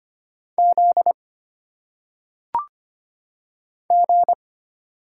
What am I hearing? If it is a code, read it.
Morse code, heard as a keyed tone, about 25 wpm, 700 Hz. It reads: Z G
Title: Single Letters - Phonetic Pronunciation 25wpm